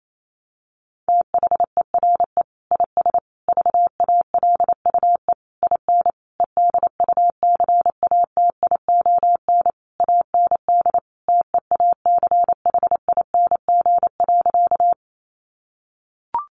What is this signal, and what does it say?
Morse code, heard as a keyed tone, about 28 wpm, 700 Hz. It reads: THERE IS VALUE IN EDUCATION AND TEACHING.
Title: Sentences from Top 1100 Words 28wpm